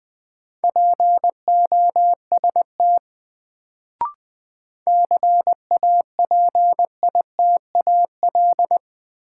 Morse code, heard as a keyed tone, about 20 wpm, 700 Hz. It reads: POST CAPITAL
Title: Sentences from Top 1000 Words - New Word Drill - 20wpm